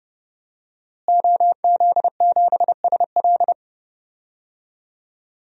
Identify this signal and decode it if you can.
Morse code, heard as a keyed tone, about 30 wpm, 700 Hz. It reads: OZ7SL
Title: DXCC-100 Call Signs - 5 Characters - Repeated 2x - 30wpm